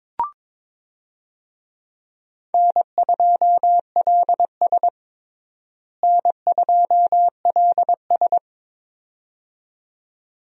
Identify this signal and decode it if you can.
Morse code, heard as a keyed tone, about 22 wpm, 700 Hz. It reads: N2LS N2LS
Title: USA Call Signs - 4 Characters - Repeated 2x - 22wpm